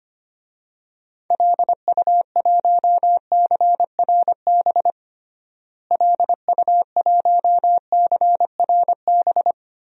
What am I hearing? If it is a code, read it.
Morse code, heard as a keyed tone, about 25 wpm, 700 Hz. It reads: LU1CRB LU1CRB
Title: DXCC-100 Call Signs - 6 Characters - Repeated 2x - 25wpm